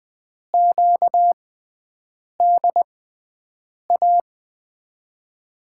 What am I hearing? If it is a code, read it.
Morse code, heard as a keyed tone, about 20 wpm, spaced as for 7 wpm, 700 Hz. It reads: QDA
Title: USA Call Signs - 6 Characters - 6x Character Spacing - 20wpm